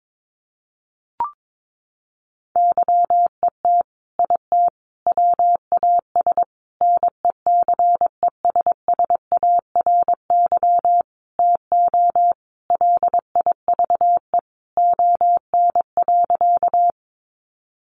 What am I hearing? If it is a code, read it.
Morse code, heard as a keyed tone, about 22 wpm, 700 Hz. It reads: YET IT WAS NECESSARY TO LIVE ON.
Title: Sentences from Top 800 Words 22wpm